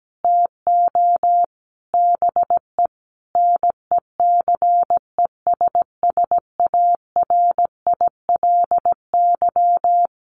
Morse code, heard as a keyed tone, about 17 wpm, 700 Hz. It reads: TO BE NECESSARILY
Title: Sentences from Top 1400 Words 17wpm